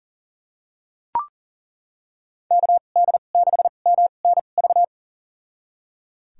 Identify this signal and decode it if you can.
Morse code, heard as a keyed tone, about 39 wpm, spaced as for 28 wpm, 700 Hz. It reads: KD6MNV
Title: USA Call Signs - 6 Characters - 2x Character Spacing - 40wpm